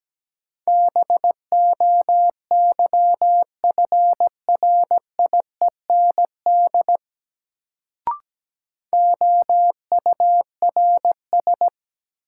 Morse code, heard as a keyed tone, about 17 wpm, 700 Hz. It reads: BOYFRIEND OURS
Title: Sentences from Top 1000 Words - Review Part 5 - New Word Drill - 17wpm